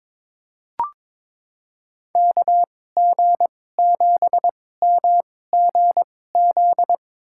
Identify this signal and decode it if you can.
Morse code, heard as a keyed tone, about 22 wpm, spaced as for 16 wpm, 700 Hz. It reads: KG7MGZ
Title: USA Call Signs - 2x Character Spacing - 22wpm